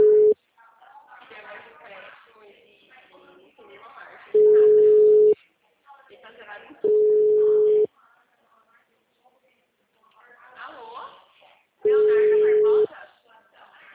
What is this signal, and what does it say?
Ringback tone (Europe, 425 Hz), with speech.